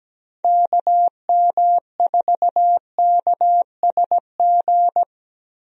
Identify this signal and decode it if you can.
Morse code, heard as a keyed tone, about 17 wpm, 700 Hz. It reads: KM4KSG
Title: USA Call Signs - 6 Characters - 17wpm